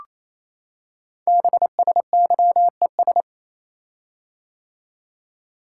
Morse code, heard as a keyed tone, about 29 wpm, 700 Hz. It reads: BSYES
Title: Code Groups - 5 Letters - 28wpm